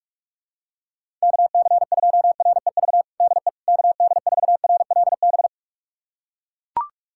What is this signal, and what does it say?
Morse code, heard as a keyed tone, about 41 wpm, 700 Hz. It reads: KC2REV DE KD4RLB
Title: USA Callsign DE Callsign - 6 Characters - 45wpm